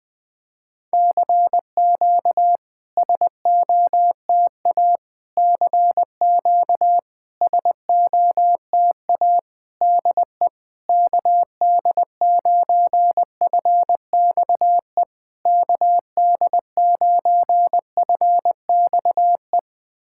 Morse code, heard as a keyed tone, about 21 wpm, 700 Hz. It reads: CQ SOTA CQ SOTA DE KD9FXE KD9FXE
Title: SOTA QSO Practice 20wpm